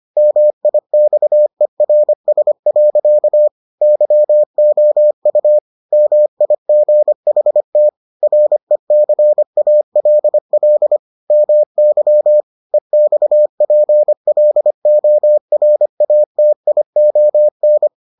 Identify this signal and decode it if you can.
Morse code, heard as a keyed tone, about 25 wpm, 600 Hz. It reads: MIXERS. YOU MIGHT RECALL MY EXPLORATION